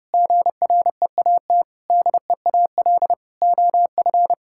Morse code, heard as a keyed tone, about 32 wpm, 700 Hz. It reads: GREAT DEAL OF
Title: Sentences from Top 1000 Words - Review Part 6 - 30wpm